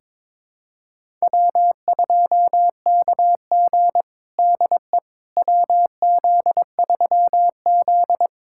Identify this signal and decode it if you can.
Morse code, heard as a keyed tone, about 22 wpm, 700 Hz. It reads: W2KG DE WZ3Z
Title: USA Callsign DE Callsign - 4 Characters - 22wpm